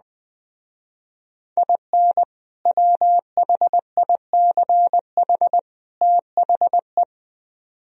Morse code, heard as a keyed tone, about 20 wpm, 700 Hz. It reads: IN WHICH THE
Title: Speed-Racing - Sets of 3 Words 17wpm